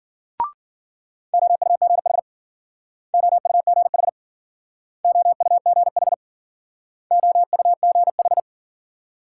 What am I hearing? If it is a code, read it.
Morse code, heard as a keyed tone, about 54 wpm, 700 Hz. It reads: OUGH OUGH OUGH OUGH